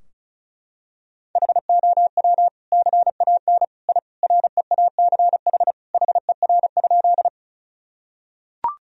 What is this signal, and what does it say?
Morse code, heard as a keyed tone, about 34 wpm, 700 Hz. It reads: HOW CAN I REACH HER?
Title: Sentences from Top 1000 Words - Review Part 2 - 35wpm